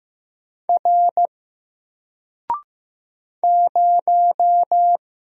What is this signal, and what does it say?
Morse code, heard as a keyed tone, about 15 wpm, 700 Hz. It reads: R 0
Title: Single Character (Letter:Number:Punctuation:Prosign) 15wpm